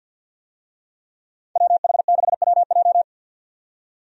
Morse code, heard as a keyed tone, about 50 wpm, 700 Hz. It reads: WH6WJ
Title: USA Call Signs - 5 Characters - 50wpm